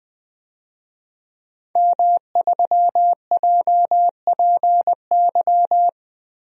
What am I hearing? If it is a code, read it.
Morse code, heard as a keyed tone, about 21 wpm, 700 Hz. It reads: M3JPY
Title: DXCC-100 Call Signs - 5 Characters - Repeated 3x - 20wpm